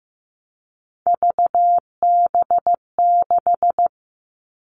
Morse code, heard as a keyed tone, about 15 wpm, 700 Hz. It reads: VB6
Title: All Cut Numbers - 3-Digit 15wpm